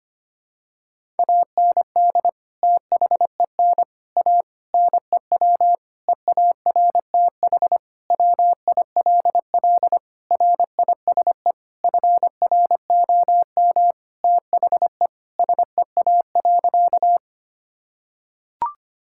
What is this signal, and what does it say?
Morse code, heard as a keyed tone, about 25 wpm, 700 Hz. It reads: AND THEN A NEW EARTH WILL RISE FROM THE SEA.
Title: Sentences from Top 1000 Words - Review Part 6 - 25wpm